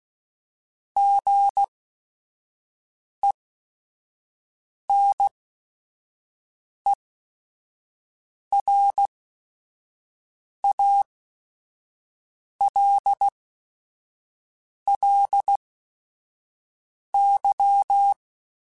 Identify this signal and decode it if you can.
Morse code, heard as a keyed tone, about 16 wpm, spaced as for 5 wpm, 800 Hz. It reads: GENERALLY
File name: Weather Sets / Weather Sets-mp3s / Weather 5 5 @5